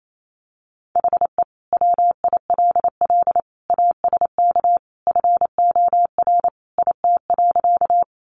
Morse code, heard as a keyed tone, about 28 wpm, 700 Hz. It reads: HE WILL ASK FOR IT.
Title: Sentences from Top 200 Words 28wpm